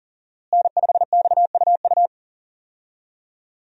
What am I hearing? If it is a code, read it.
Morse code, heard as a keyed tone, about 40 wpm, 700 Hz. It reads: N5XUU
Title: USA Call Signs - 5 Characters - Repeated 2x - 40wpm